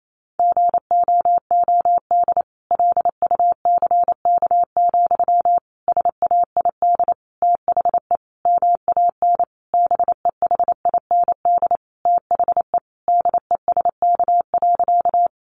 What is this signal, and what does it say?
Morse code, heard as a keyed tone, about 28 wpm, 700 Hz. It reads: GOOD LUCK, SAID THE MAN BEHIND THE DESK.
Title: A Sound Of Thunder 28wpm